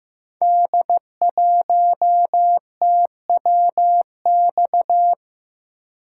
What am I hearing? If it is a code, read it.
Morse code, heard as a keyed tone, about 15 wpm, 700 Hz. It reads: D1TWX
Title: Code Groups - 5 Letter:Numbers - 15wpm